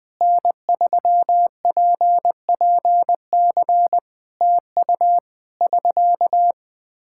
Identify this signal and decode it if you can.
Morse code, heard as a keyed tone, about 20 wpm, 700 Hz. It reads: N3PPC TU <SK>
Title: Practice QSO - Exchange 4 - Ending - 20wpm